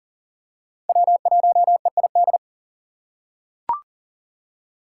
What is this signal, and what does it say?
Morse code, heard as a keyed tone, about 39 wpm, 700 Hz. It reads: W1EID
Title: USA Call Signs - 5 Characters - 2x Character Spacing - 40wpm